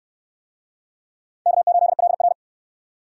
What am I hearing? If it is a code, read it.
Morse code, heard as a keyed tone, about 63 wpm, 700 Hz. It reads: K8LR